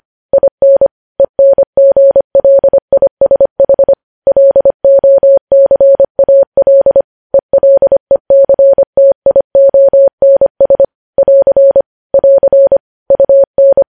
Morse code, heard as a keyed tone, about 25 wpm, 550 Hz. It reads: IN ENGLISH LOCAL ELECTIONS <AR> <AR> UN
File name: podcast / News headlines-1x-25x25